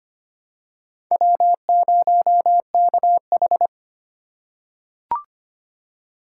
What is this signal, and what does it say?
Morse code, heard as a keyed tone, about 25 wpm, 700 Hz. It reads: W0KH
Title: USA Call Signs - 6x Character Spacing - 25wpm